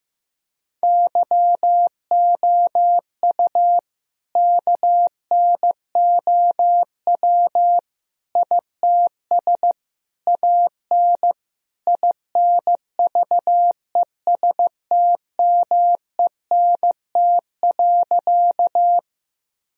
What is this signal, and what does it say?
Morse code, heard as a keyed tone, about 15 wpm, 700 Hz. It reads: YOU KNOW ITS AN INVESTMENT.
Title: Sentences from Top 1000 Words - Review Part 4 - 15wpm